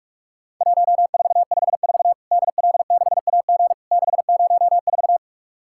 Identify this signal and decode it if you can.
Morse code, heard as a keyed tone, about 46 wpm, 700 Hz. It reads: 1454 DL6AG 604
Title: Worked All Europe DX Contest - QTC Exchange - 45wpm